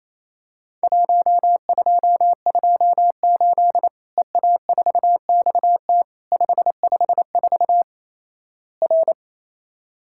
Morse code, heard as a keyed tone, about 28 wpm, 700 Hz. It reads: 1228 EA4XT 554 R
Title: Worked All Europe DX Contest - QTC Exchange - 28wpm